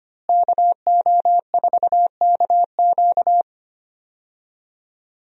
Morse code, heard as a keyed tone, about 25 wpm, 700 Hz. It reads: KO4KQ
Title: USA Call Signs - 5 Characters - Repeated 3x - 25wpm